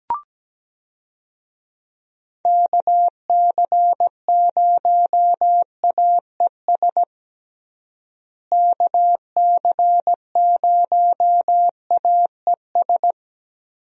Morse code, heard as a keyed tone, about 17 wpm, 700 Hz. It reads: KC0AES KC0AES
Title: USA Call Signs - 6 Characters - Repeated 3x - 17wpm